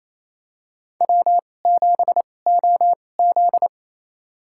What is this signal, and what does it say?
Morse code, heard as a keyed tone, about 28 wpm, spaced as for 20 wpm, 700 Hz. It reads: W7OZ
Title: USA Call Signs - 4 Characters - 2x Character Spacing - 28wpm